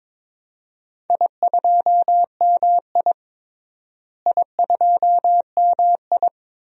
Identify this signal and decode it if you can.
Morse code, heard as a keyed tone, about 22 wpm, 700 Hz. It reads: I2MI I2MI
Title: DXCC-100 Call Signs - 4 Characters - Repeated 2x - 22wpm